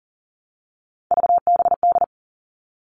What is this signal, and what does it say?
Morse code, heard as a keyed tone, about 40 wpm, 700 Hz. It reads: VBD
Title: All Cut Numbers - 3-Digit 40wpm